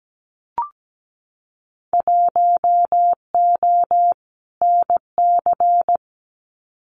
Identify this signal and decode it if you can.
Morse code, heard as a keyed tone, about 17 wpm, 700 Hz. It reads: 1O NC
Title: Winter Field Day Exchanges 17wpm